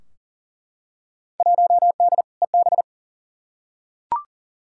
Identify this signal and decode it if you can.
Morse code, heard as a keyed tone, about 40 wpm, 700 Hz. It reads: 1D EB
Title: Field Day Exchanges - 40wpm